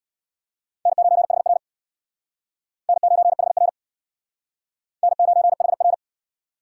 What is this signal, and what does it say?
Morse code, heard as a keyed tone, about 62 wpm, 700 Hz. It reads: N9HR N9HR N9HR
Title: Speed Racing - USA 1x2 Call Signs 50wpm